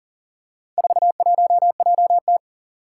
Morse code, heard as a keyed tone, about 40 wpm, 700 Hz. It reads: 41JT
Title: Four Challenging Letter-Number Combinations 40wpm